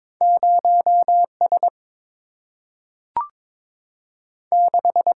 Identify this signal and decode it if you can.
Morse code, heard as a keyed tone, about 22 wpm, 700 Hz. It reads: 0 S 6
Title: Single-Letter Single-Number 22wpm